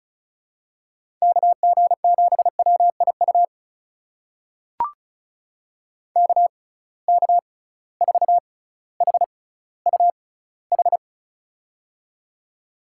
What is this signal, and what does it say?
Morse code, heard as a keyed tone, about 36 wpm, spaced as for 18 wpm, 700 Hz. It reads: KG7WIU KK4HUH
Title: USA Call Signs - 6 Characters - 6x Character Spacing - 35wpm